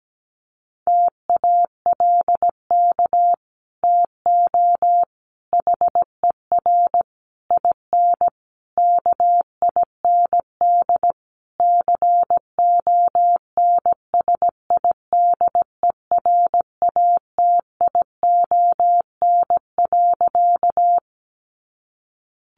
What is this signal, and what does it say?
Morse code, heard as a keyed tone, about 17 wpm, 700 Hz. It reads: TALK TO HER IN KIND CONSIDERATION.
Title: Sentences from Top 1500 Words 17wpm